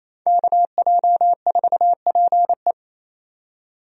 Morse code, heard as a keyed tone, about 27 wpm, 700 Hz. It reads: KJ4PE
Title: USA Call Signs - 5 Characters - 28wpm